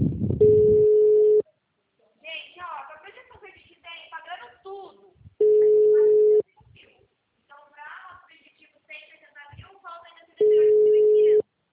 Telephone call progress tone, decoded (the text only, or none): ringback tone